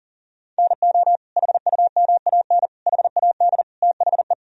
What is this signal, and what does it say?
Morse code, heard as a keyed tone, about 40 wpm, 700 Hz. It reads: NO HUMAN HAD THE
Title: Sentences from Top 1000 Words - Review Part 3 - 40wpm